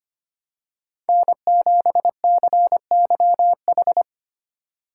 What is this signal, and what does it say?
Morse code, heard as a keyed tone, about 25 wpm, 700 Hz. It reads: N7CYH
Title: USA Call Signs - 5 Characters - 6x Character Spacing - 25wpm